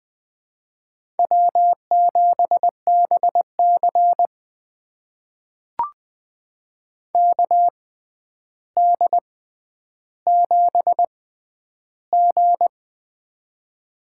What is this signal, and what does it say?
Morse code, heard as a keyed tone, about 20 wpm, spaced as for 10 wpm, 700 Hz. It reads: W7BC KD7G
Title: USA Call Signs - 4 Characters - 6x Character Spacing - 20wpm